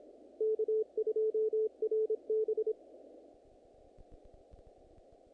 Morse code, heard as a keyed tone, about 26 wpm, 440 Hz, with radio noise. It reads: K2RB
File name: contests / naqp-cw-k5tr-op-k5pi- -2000